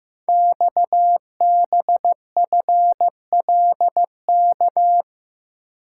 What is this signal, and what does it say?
Morse code, heard as a keyed tone, about 15 wpm, 700 Hz. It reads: XBFLK
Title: Code Groups - 5 Letters - 15wpm